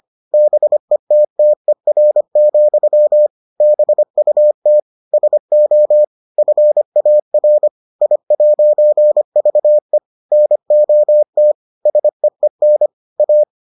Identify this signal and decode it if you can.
Morse code, heard as a keyed tone, about 25 wpm, 600 Hz. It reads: BETTER, BUT SO FAR I'VE NOT SEEN A